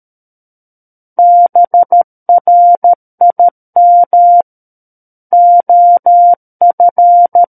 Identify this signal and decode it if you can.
Morse code, heard as a keyed tone, about 13 wpm, 700 Hz. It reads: BRIM OF
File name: Book One / 04-Chapter 4 - 13 WPM